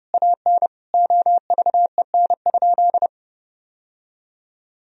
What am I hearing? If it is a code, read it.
Morse code, heard as a keyed tone, about 30 wpm, 700 Hz. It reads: AN OVEN?
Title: Sentences from Top 1000 Words - Review Part 4 - 30wpm